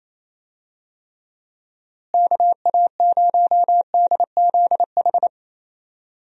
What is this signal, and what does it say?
Morse code, heard as a keyed tone, about 28 wpm, 700 Hz. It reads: KA0DZH
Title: USA Call Signs - 6 Characters - Repeated 3x - 28wpm